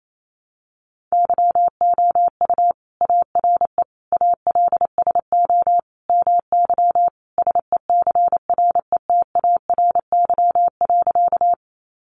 Morse code, heard as a keyed tone, about 28 wpm, 700 Hz. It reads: YOU ARE ALSO MY SECRETARY.
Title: Sentences from Top 1300 Words 28wpm